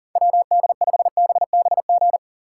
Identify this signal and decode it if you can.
Morse code, heard as a keyed tone, about 50 wpm, 700 Hz. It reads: WD5BBG
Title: Speed Racing - USA 2x3 Call Signs 40wpm